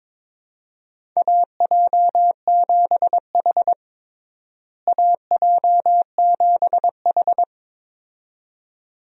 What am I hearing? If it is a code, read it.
Morse code, heard as a keyed tone, about 22 wpm, 700 Hz. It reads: AJ7H AJ7H
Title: USA Call Signs - 4 Characters - Repeated 2x - 22wpm